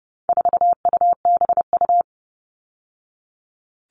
Morse code, heard as a keyed tone, about 30 wpm, 700 Hz. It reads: 4UBU
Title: All Cut Numbers - 4-Digit 30wpm